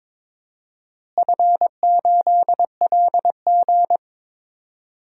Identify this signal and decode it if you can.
Morse code, heard as a keyed tone, about 22 wpm, 700 Hz. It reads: F8LG
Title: DXCC-100 Call Signs - 4 Characters - Repeated 2x - 22wpm